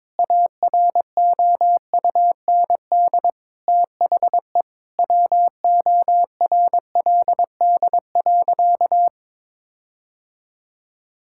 Morse code, heard as a keyed tone, about 22 wpm, 700 Hz. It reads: AROUND THE WORLD.
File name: Sentences with 7-Word Phrases - 22wpm